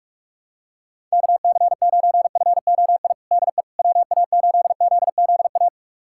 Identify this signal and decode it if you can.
Morse code, heard as a keyed tone, about 41 wpm, 700 Hz. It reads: KC9FOI DE WA8ZZA
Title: USA Callsign DE Callsign - 6 Characters - 45wpm